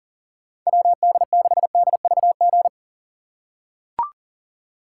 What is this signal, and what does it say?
Morse code, heard as a keyed tone, about 40 wpm, 700 Hz. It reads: WD6DVG
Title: USA Call Signs - 6 Characters - 3x Character Spacing - 40wpm